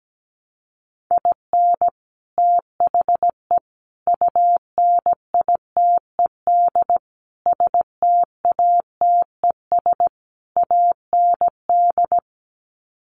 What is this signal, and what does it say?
Morse code, heard as a keyed tone, about 17 wpm, 700 Hz. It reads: IN THE UNITED STATES AND
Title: Sets of 5 Words 17wpm